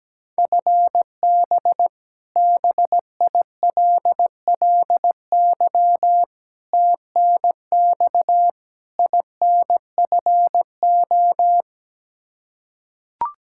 Morse code, heard as a keyed tone, about 17 wpm, 700 Hz. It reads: FB BILLY TNX INFO
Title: K1USN SST QSO Practice 17wpm